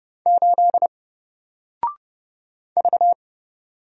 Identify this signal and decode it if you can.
Morse code, heard as a keyed tone, about 30 wpm, 700 Hz. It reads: 8 V
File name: Single-Letter-Number-30wpm